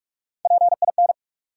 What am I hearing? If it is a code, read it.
Morse code, heard as a keyed tone, about 44 wpm, 700 Hz. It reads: PIN
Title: Three Letter Words - 45wpm